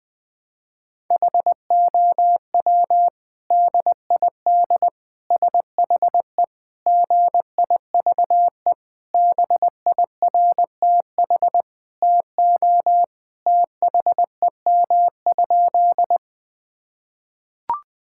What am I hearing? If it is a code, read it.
Morse code, heard as a keyed tone, about 20 wpm, 700 Hz. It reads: HOW DID SHE GIVE BIRTH TO THEM?
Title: Sentences from Top 1000 Words - Review Part 5 - 20wpm